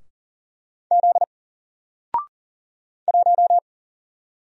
Morse code, heard as a keyed tone, about 39 wpm, 700 Hz. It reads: Z 1
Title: Single Character (Letter:Number:Punctuation:Prosign) 40wpm